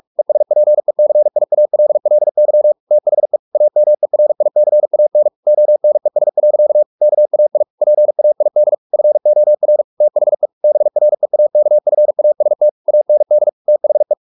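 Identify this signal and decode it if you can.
Morse code, heard as a keyed tone, about 44 wpm, 600 Hz. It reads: ESPECIALLY THE AMERICAN ONES. KAI PAID FOR THE BREAKFAST AND THE
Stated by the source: Morse practice at 45 wpm, the novel Never by Ken Follett in English, from CW med Gnister.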